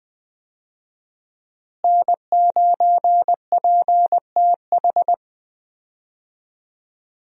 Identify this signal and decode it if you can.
Morse code, heard as a keyed tone, about 20 wpm, 700 Hz. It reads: N9PTH
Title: USA Call Signs - 5 Characters - Repeated 2x - 20wpm